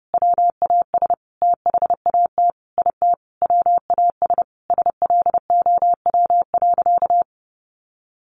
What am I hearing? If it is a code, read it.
Morse code, heard as a keyed tone, about 28 wpm, 700 Hz. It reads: WAS THAT IT WAS SLOW.
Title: Sentences from Top 1400 Words 30wpm